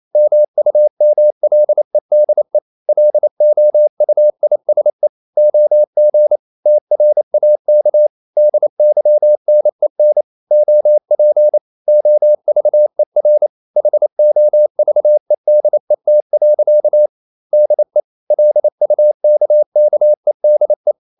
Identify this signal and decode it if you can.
Morse code, heard as a keyed tone, about 28 wpm, 600 Hz. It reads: MUMLEDE LOUISE OG TRAK DYNEN OP OVER HOVEDET. DE LUKKEDE
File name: Ferie QRQ 28wpm